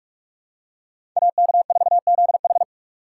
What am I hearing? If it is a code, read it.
Morse code, heard as a keyed tone, about 44 wpm, 700 Hz. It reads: AK4ZH
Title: USA Call Signs - 5 Characters - 6x Character Spacing - 45wpm